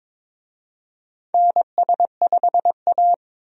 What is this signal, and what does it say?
Morse code, heard as a keyed tone, about 22 wpm, 700 Hz. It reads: NS5A
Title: USA Call Signs - 4 Characters - 22wpm